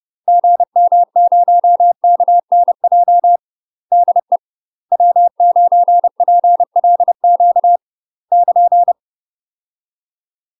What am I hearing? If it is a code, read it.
Morse code, heard as a keyed tone, about 30 wpm, 700 Hz. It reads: GM0KNJ DE W9PLQ <KN>
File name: UK MASTER QSOs TRAINING / 30WPM / UK MASTER QSOs 0042